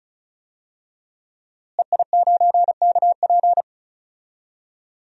Morse code, heard as a keyed tone, about 36 wpm, 700 Hz. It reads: EI9KP
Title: DXCC-100 Call Signs - 5 Characters - Repeated 2x - 35wpm